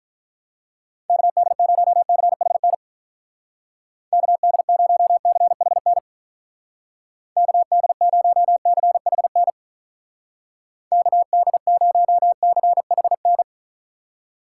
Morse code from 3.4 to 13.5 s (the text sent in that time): KD0CHN KD0CHN KD0CHN